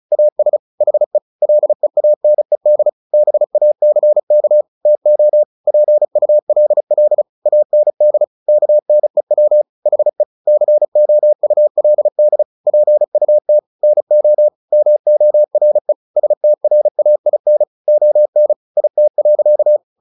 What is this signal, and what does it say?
Morse code, heard as a keyed tone, about 34 wpm, 600 Hz. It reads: AS HE LEANED BACK TO PULL AND KNEW HE COULD PUT NO MORE STRAIN ON IT.
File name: the old man and the sea / the old man and the sea 1 35wpm